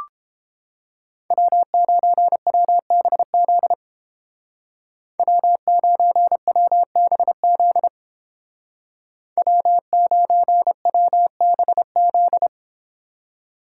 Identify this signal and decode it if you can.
Morse code, heard as a keyed tone, about 30 wpm, 700 Hz. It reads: W9WBZ W9WBZ W9WBZ